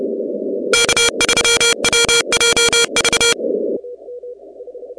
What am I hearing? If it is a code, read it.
Morse code, heard as a keyed tone, about 29 wpm, 2350 Hz, with noise. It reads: K3WJV
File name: contests / 2019-cw-wpx-kt5j-k5pi- -2330 0-17